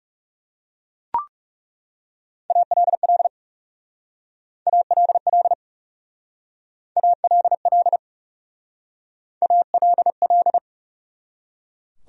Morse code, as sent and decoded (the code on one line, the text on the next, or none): .- .-.. .-.. / .- .-.. .-.. / .- .-.. .-.. / .- .-.. .-..
ALL ALL ALL ALL